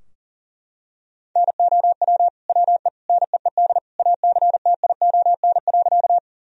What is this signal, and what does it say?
Morse code, heard as a keyed tone, about 40 wpm, 700 Hz. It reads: NOW WE NEED ACTION.
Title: Sentences from Top 1000 Words - Review Part 4 - 40wpm